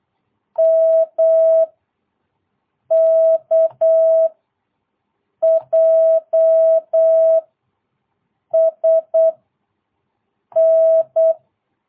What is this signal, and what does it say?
Morse code, heard as a keyed tone, about 8 wpm, 650 Hz. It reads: M K J S N
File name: jh8cbh / 2022cw